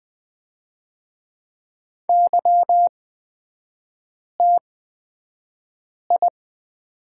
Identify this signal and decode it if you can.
Morse code, heard as a keyed tone, about 20 wpm, spaced as for 5 wpm, 700 Hz. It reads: YTI
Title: 236 Lesson 38e - Letter q - Three Characters 20wpm